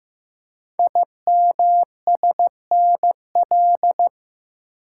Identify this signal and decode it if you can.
Morse code, heard as a keyed tone, about 15 wpm, 700 Hz. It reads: IMSNL
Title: Code Groups - 5 Letters - 15wpm